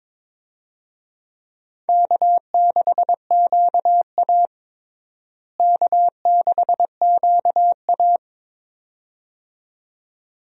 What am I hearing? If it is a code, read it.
Morse code, heard as a keyed tone, about 22 wpm, 700 Hz. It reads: K6QA K6QA
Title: USA Call Signs - 4 Characters - Repeated 2x - 22wpm